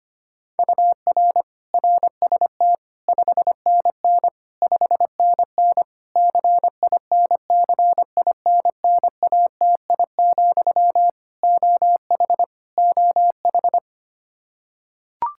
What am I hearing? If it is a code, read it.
Morse code, heard as a keyed tone, about 25 wpm, 700 Hz. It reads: UR RST 5NN 5NN CINCINNATI, OH OH
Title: Practice QSO - Exchange 1 - Beginning - 25wpm